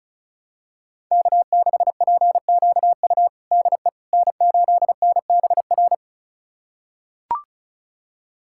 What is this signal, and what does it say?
Morse code, heard as a keyed tone, about 36 wpm, 700 Hz. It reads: K6PQU DE N8NBR
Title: USA Callsign DE Callsign - 5 Characters - 35wpm